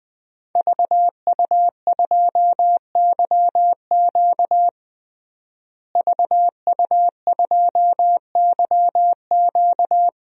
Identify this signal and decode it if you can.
Morse code, heard as a keyed tone, about 20 wpm, 700 Hz. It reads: VU2YQ VU2YQ
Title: DXCC-100 Call Signs - 5 Characters - Repeated 3x - 20wpm